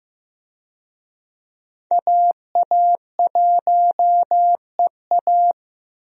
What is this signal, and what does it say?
Morse code, heard as a keyed tone, about 15 wpm, 700 Hz. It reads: AA1EA
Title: USA Call Signs - 5 Characters - Repeated 3x - 15wpm